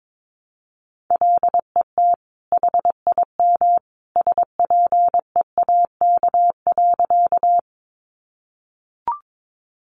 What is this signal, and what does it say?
Morse code, heard as a keyed tone, about 22 wpm, 700 Hz. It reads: LET HIM SPEAK.
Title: Sentences from Top 1100 Words 22wpm